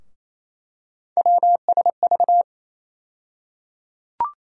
Morse code, heard as a keyed tone, about 28 wpm, 700 Hz. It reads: WSV
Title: USA Call Sign Postfixes - 3 Letters 28wpm